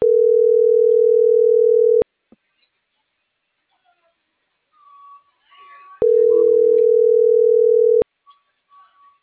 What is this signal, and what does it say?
Ringback tone (North America), 440 + 480 Hz.